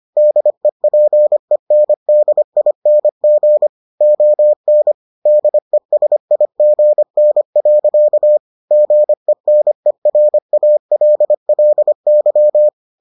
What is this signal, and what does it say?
Morse code, heard as a keyed tone, about 25 wpm, 600 Hz. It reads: DEPENDING ON DESIGN. GENERALLY